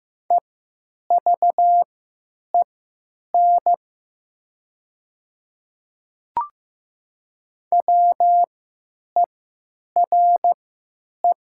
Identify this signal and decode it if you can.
Morse code, heard as a keyed tone, about 15 wpm, spaced as for 9 wpm, 700 Hz. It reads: EVEN WERE
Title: 3x Character Spacing - Top 200 Words - 15wpm